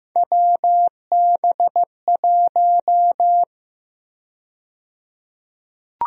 Morse code, heard as a keyed tone, about 15 wpm, 700 Hz. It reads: WB1
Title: USA Call Sign Prefixes - 15wpm